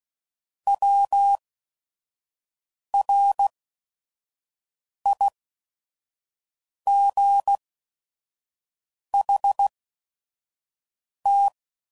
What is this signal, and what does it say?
Morse code, heard as a keyed tone, about 16 wpm, spaced as for 5 wpm, 800 Hz. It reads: WRIGHT
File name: Mixed / Misc / Words